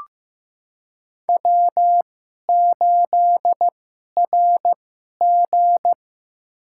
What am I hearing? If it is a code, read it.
Morse code, heard as a keyed tone, about 15 wpm, spaced as for 11 wpm, 700 Hz. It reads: W8RG
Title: USA Call Signs - 4 Characters - 2x Character Spacing - 15wpm